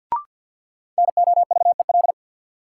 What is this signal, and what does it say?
Morse code, heard as a keyed tone, about 47 wpm, 700 Hz. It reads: NOVEL
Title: Sentences from Top 1300 Words - New Word Drill - 50wpm